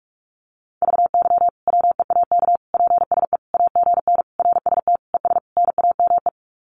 Morse code, heard as a keyed tone, about 44 wpm, 700 Hz. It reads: VY WEAK PSE AGN RST ES NAME
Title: Common QSO Phrases 45wpm